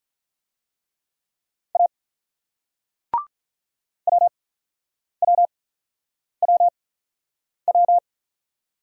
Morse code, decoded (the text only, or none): A WWWW